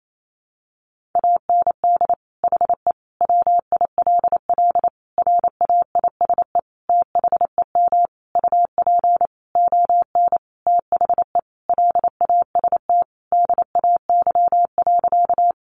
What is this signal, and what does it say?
Morse code, heard as a keyed tone, about 28 wpm, 700 Hz. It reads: AND HE WILL RAISE THEM UP ON THE LAST DAY.
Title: Sentences from Top 1400 Words 28wpm